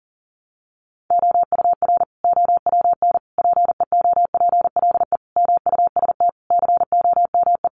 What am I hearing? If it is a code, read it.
Morse code, heard as a keyed tone, about 40 wpm, 700 Hz. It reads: OUR OWN PEOPLE MUST COME
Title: Sentences from Top 300 Words 40wpm